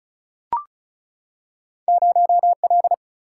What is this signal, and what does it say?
Morse code, heard as a keyed tone, about 36 wpm, 700 Hz. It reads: 0 L
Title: Single-Letter Single-Number 35wpm